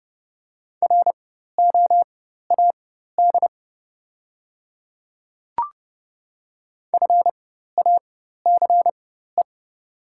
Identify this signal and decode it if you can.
Morse code, heard as a keyed tone, about 30 wpm, spaced as for 14 wpm, 690 Hz. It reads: ROAD FACE
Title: 4x Character Spacing - Top 500 Words - 30wpm